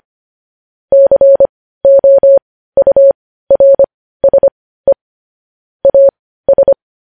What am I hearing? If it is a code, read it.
Morse code, heard as a keyed tone, about 25 wpm, spaced as for 15 wpm, 550 Hz. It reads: COURSE AS
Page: news-headlines-in-morse-code-at-15-wpm